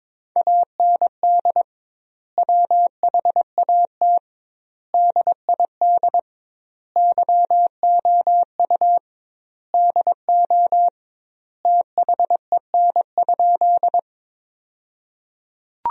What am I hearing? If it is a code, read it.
Morse code, heard as a keyed tone, about 22 wpm, 700 Hz. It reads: AND WHAT DID YOU DO THEN?
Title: 2x Word-Spacing - Sentences from Top 100 Words - 22wpm